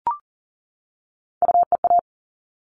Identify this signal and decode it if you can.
Morse code, heard as a keyed tone, about 40 wpm, 700 Hz. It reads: UEA
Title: All Cut Numbers - 3-Digit 40wpm